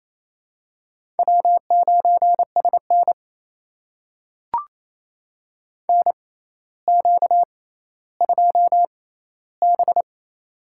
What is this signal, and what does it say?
Morse code, heard as a keyed tone, about 28 wpm, spaced as for 14 wpm, 700 Hz. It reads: W9SN NQ2B
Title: USA Call Signs - 4 Characters - 6x Character Spacing - 28wpm